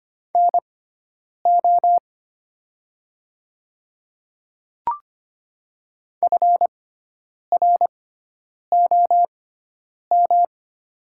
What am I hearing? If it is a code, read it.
Morse code, heard as a keyed tone, about 25 wpm, spaced as for 9 wpm, 700 Hz. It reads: NO FROM
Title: 6x Character Spacing - Top 100 Words - 25wpm